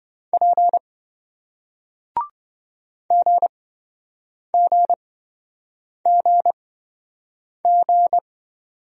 Morse code, heard as a keyed tone, about 27 wpm, spaced as for 7 wpm, 700 Hz. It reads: P GGGG